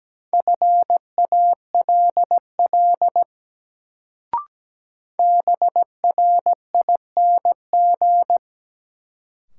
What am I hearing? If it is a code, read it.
Morse code, heard as a keyed tone, about 17 wpm, 700 Hz. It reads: FALL BRING
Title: Sentences from Top 500 Words - New Word Drill - 17wpm